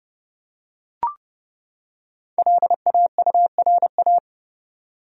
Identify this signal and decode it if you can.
Morse code, heard as a keyed tone, about 30 wpm, 700 Hz. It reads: LAURA
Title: USA Common Names - 30wpm